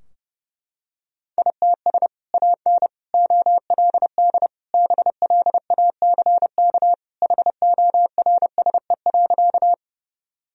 Morse code, heard as a keyed tone, about 30 wpm, 700 Hz. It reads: ITS AN OLD BLACK HORSE.
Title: Sentences from Top 1000 Words - Review Part 2 - 30wpm